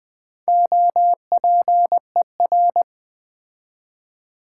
Morse code, heard as a keyed tone, about 20 wpm, 700 Hz. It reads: OPER
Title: Build A Word - 4 Letters At A Time - 11-Letter Words - 20wpm